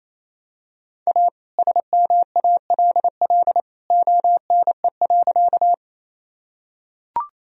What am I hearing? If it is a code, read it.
Morse code, heard as a keyed tone, about 28 wpm, 700 Hz. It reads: A SMALL ONE.
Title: Sentences from Top 1000 Words - Review Part 1 - 28wpm